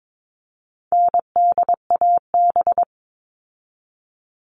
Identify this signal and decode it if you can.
Morse code, heard as a keyed tone, about 22 wpm, 700 Hz. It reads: NDAB
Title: All Cut Numbers - 4-Digit 22wpm